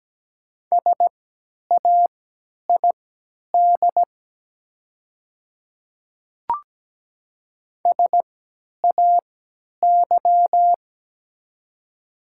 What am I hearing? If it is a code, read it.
Morse code, heard as a keyed tone, about 17 wpm, spaced as for 10 wpm, 700 Hz. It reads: SAID SAY
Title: 3x Character Spacing - Top 200 Words - 17wpm